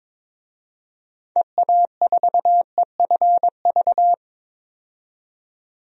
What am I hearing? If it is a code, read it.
Morse code, heard as a keyed tone, about 22 wpm, 700 Hz. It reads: EA4EFV
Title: DXCC-100 Call Signs - 6 Characters - 22wpm